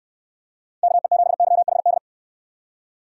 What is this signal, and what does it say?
Morse code, heard as a keyed tone, about 64 wpm, 700 Hz. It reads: KE7YHD